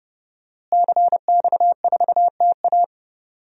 Morse code, heard as a keyed tone, about 30 wpm, 700 Hz. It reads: CX4TA
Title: DXCC-100 Call Signs - 5 Characters - Repeated 3x - 30wpm